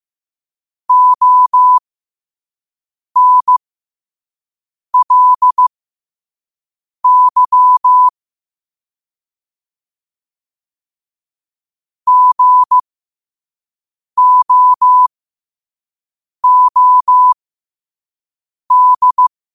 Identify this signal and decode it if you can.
Morse code, heard as a keyed tone, about 15 wpm, spaced as for 5 wpm, 1000 Hz. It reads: ONLY GOOD
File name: qotdfiles / 5wpm